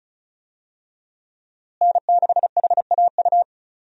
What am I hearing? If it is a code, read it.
Morse code, heard as a keyed tone, about 34 wpm, 700 Hz. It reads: N6HAU
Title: USA Call Signs - 5 Characters - Repeated 2x - 35wpm